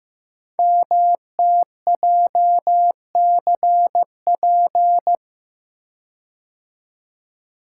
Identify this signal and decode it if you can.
Morse code, heard as a keyed tone, about 15 wpm, 700 Hz. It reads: MTJCP
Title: Code Groups - 5 Letters - 15wpm